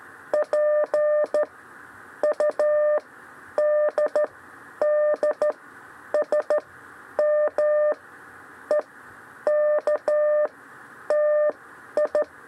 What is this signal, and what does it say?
Morse code, heard as a keyed tone, about 13 wpm, spaced as for 10 wpm, 600 Hz. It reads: P U D D S M E K TI
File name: 6 lezione cw 2